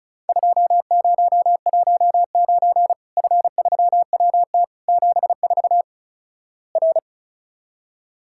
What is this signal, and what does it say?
Morse code, heard as a keyed tone, about 36 wpm, 700 Hz. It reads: 2019 F3WT 74 R
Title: Worked All Europe DX Contest - QTC Exchange - 35wpm